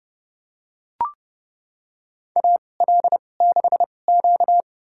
Morse code, heard as a keyed tone, about 32 wpm, spaced as for 22 wpm, 700 Hz. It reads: AL6Q
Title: USA Call Signs - 4 Characters - 2x Character Spacing - 30wpm